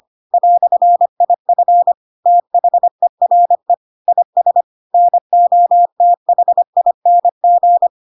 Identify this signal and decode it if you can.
Morse code, heard as a keyed tone, about 25 wpm, 700 Hz. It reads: "IF THERE IS NOTHING
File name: Pinocchio / Pinocchio ch7 25 25 ik1hge